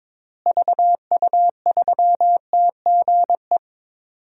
Morse code, heard as a keyed tone, about 22 wpm, 700 Hz. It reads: VU3TGE
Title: DXCC-100 Call Signs - 6 Characters - Repeated 2x - 22wpm